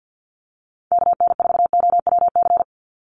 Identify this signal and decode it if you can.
Morse code, heard as a keyed tone, about 50 wpm, 700 Hz. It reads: KN4OWC
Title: USA Call Signs 50wpm